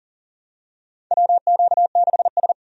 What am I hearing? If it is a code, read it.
Morse code, heard as a keyed tone, about 37 wpm, 700 Hz. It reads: WQ6S
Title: USA Call Signs - 4 Characters - 40wpm